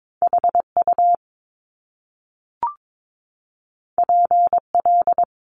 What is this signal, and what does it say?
Morse code, heard as a keyed tone, about 22 wpm, 700 Hz. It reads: HU PL
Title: Top Two Letter Combinations in Words 22wpm